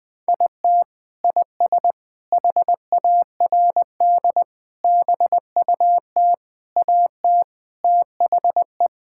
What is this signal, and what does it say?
Morse code, heard as a keyed tone, about 20 wpm, 700 Hz. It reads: IT IS HARD BUT AT THE
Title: Sentences with 5-Word Phrases - Spoken Beginning - 20wpm